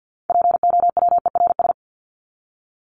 Morse code, heard as a keyed tone, about 50 wpm, 700 Hz. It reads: POWERS
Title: ES:S Suffix - Top 100 Words with Base Words 50wpm